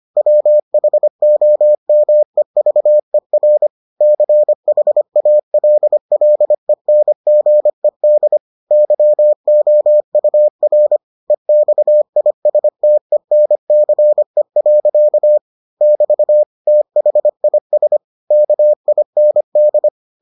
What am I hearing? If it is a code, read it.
Morse code, heard as a keyed tone, about 25 wpm, 600 Hz. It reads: WHOMEVER CHALLENGED YOUR EXISTENCE. <BT> THIS KIND